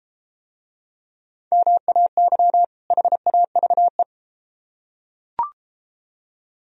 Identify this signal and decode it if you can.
Morse code, heard as a keyed tone, about 32 wpm, 700 Hz. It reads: MAY HAVE